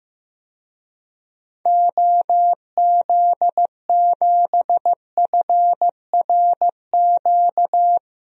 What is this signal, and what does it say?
Morse code, heard as a keyed tone, about 15 wpm, 700 Hz. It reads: OZ7FRQ
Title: DXCC-100 Call Signs - 6 Characters - Repeated 3x - 15wpm